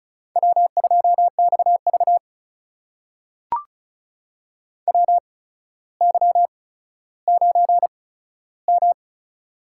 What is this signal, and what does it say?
Morse code, heard as a keyed tone, about 36 wpm, spaced as for 15 wpm, 700 Hz. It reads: W2XV WY9M
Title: USA Call Signs - 4 Characters - 8x Character Spacing - 35wpm